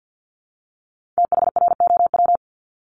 Morse code, heard as a keyed tone, about 50 wpm, 700 Hz. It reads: THROW
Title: Multi-letter Phonemes 50wpm